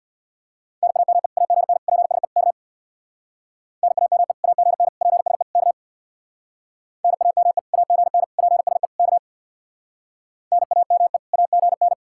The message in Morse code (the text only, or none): NAME AGN PSE K NAME AGN PSE K NAME AGN PSE K NAME AGN